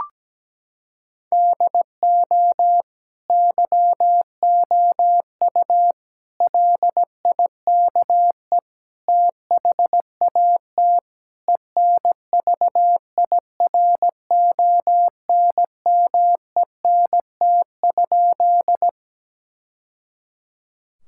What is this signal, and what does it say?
Morse code, heard as a keyed tone, about 17 wpm, 700 Hz. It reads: DO YOU LIKE THAT ENVIRONMENT?
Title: Sentences from Top 1000 Words - Review Part 5 - 17wpm